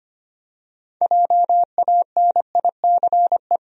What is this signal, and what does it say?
Morse code, heard as a keyed tone, about 25 wpm, 700 Hz. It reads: JANICE
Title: USA Common Names - 25wpm